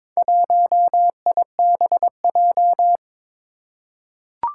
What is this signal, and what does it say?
Morse code, heard as a keyed tone, about 22 wpm, 700 Hz. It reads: 1IBJ
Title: Four Challenging Letter-Number Combinations 22wpm